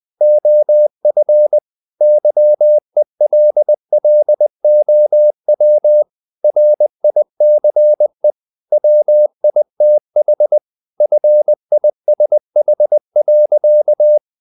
Morse code, heard as a keyed tone, about 20 wpm, 600 Hz. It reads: OF YELLOW RICE WITH FISH.
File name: the old man and the sea / the old man and the sea part 1 20wpm